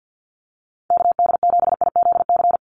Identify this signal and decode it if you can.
Morse code, heard as a keyed tone, about 49 wpm, 700 Hz. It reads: KD7IZC
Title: USA Call Signs 50wpm